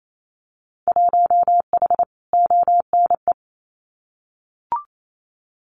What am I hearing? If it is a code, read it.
Morse code, heard as a keyed tone, about 28 wpm, 700 Hz. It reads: 1H ONE
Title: Winter Field Day Exchanges Encore 28wpm